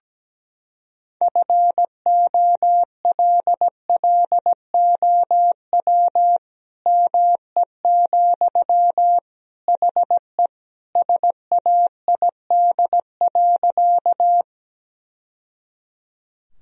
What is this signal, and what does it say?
Morse code, heard as a keyed tone, about 17 wpm, 700 Hz. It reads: FOLLOW ME, HE SAID.
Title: Sentences from Top 1000 Words - Review Part 1 - 17wpm